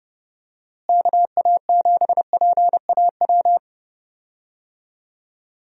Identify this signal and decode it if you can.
Morse code, heard as a keyed tone, about 32 wpm, 700 Hz. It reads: KA7PAW
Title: USA Call Signs - 6 Characters - Repeated 3x - 30wpm